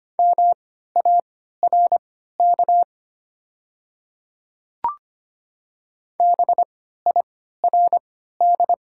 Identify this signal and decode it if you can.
Morse code, heard as a keyed tone, about 25 wpm, spaced as for 14 wpm, 700 Hz. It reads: MARK BIRD
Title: 3x Character Spacing - Top 500 Words - 25wpm